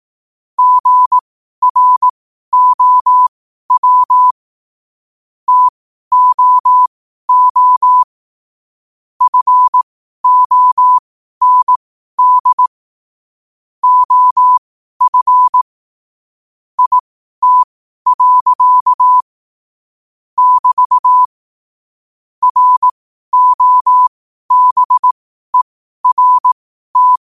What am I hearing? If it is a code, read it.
Morse code, heard as a keyed tone, about 18 wpm, spaced as for 12 wpm, 1000 Hz. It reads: GROW TOO FOND OF IT. <BT> ROBERT
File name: qotdfiles / 12wpm